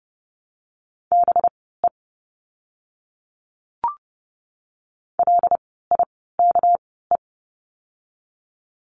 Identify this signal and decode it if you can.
Morse code, heard as a keyed tone, about 30 wpm, spaced as for 17 wpm, 700 Hz. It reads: BE LIKE
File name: course / 30:10wpm / 237 Lesson 38f - Letter q - Words 30wpm